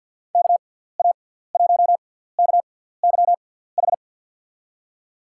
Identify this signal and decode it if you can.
Morse code, heard as a keyed tone, about 53 wpm, spaced as for 17 wpm, 700 Hz. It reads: KA1KYH
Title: USA Call Signs - 6 Characters - 6x Character Spacing - 50wpm